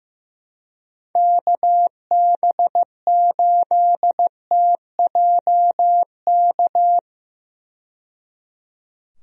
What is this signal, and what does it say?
Morse code, heard as a keyed tone, about 15 wpm, 700 Hz. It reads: KB8TJK
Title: USA Call Signs - 6 Characters - 15wpm